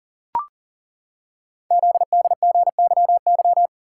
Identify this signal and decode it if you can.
Morse code, heard as a keyed tone, about 40 wpm, 700 Hz. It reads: ZDGYY
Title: Code Groups - 5 Letters - 40wpm